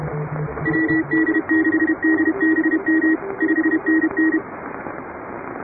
Morse code, heard as a keyed tone, about 30 wpm, 310 Hz, with radio noise. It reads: WD6DBM 5NN
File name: chatham / 2350-12CW